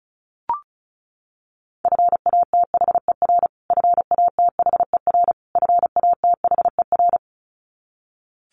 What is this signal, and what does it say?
Morse code, heard as a keyed tone, about 34 wpm, 700 Hz. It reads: FATHER FATHER FATHER
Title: Top 500 Words Repeated 35wpm